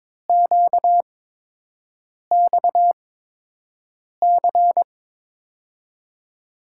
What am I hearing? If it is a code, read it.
Morse code, heard as a keyed tone, about 22 wpm, spaced as for 6 wpm, 700 Hz. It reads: QXC
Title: USA Call Signs - 5 Characters - 8x Character Spacing - 22wpm